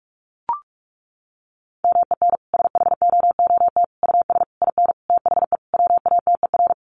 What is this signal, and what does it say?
Morse code, heard as a keyed tone, about 44 wpm, 700 Hz. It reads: MEN SHOOT US IN THE WATER
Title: Sentences from Top 1200 Words 45wpm